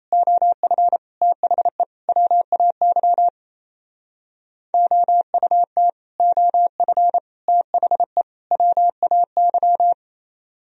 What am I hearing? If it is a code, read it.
Morse code, heard as a keyed tone, about 37 wpm, 700 Hz. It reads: OF THE WAY OUT OF THE WAY